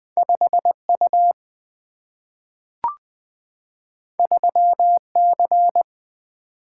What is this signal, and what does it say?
Morse code, heard as a keyed tone, about 20 wpm, 700 Hz. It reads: 5 U 3 C
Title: Single-Letter Single-Number 20wpm